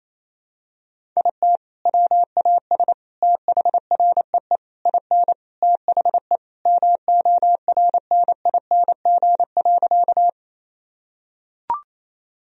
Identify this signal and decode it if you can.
Morse code, heard as a keyed tone, about 28 wpm, 700 Hz. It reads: IT WAS THREE IN THE MORNING.
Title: Sentences from Top 1000 Words - Review Part 1 - 28wpm